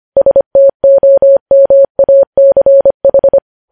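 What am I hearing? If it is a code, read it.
Morse code, heard as a keyed tone, about 25 wpm, 550 Hz. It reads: STOMACH
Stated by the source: Morse code news headlines at 25 wpm.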